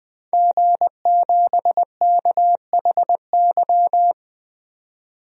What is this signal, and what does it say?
Morse code, heard as a keyed tone, about 20 wpm, 700 Hz. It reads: G7KHY
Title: DXCC-100 Call Signs - 5 Characters - 20wpm